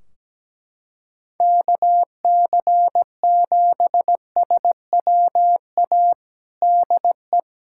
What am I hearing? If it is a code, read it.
Morse code, heard as a keyed tone, about 17 wpm, 700 Hz. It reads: KC7SWA DE
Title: Practice QSO - Exchange 4 - Ending - 17wpm